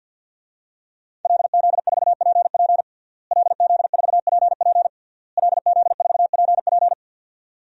Morse code, heard as a keyed tone, about 52 wpm, 700 Hz. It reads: LZ4PP LZ4PP LZ4PP
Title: DXCC-100 Call Signs - 5 Characters - Repeated 3x - 50wpm